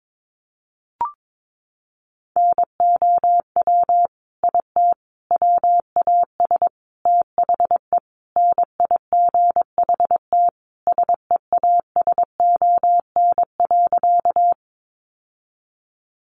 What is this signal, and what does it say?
Morse code, heard as a keyed tone, about 22 wpm, 700 Hz. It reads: NOW IT WAS THE NIGHT SEASON.
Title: Sentences from Top 1400 Words 22wpm